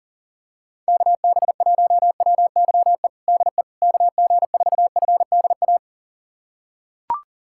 Morse code, heard as a keyed tone, about 40 wpm, 690 Hz. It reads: KB1WYE DE KG4FDA
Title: USA Callsign DE Callsign - 6 Characters - 40wpm